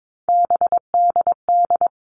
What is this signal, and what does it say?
Morse code, heard as a keyed tone, about 22 wpm, 700 Hz. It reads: BDD
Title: All Cut Numbers - 3-Digit 22wpm